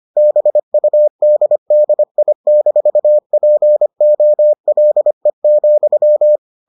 Morse code, heard as a keyed tone, about 25 wpm, 600 Hz. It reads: BUDDI-POLE,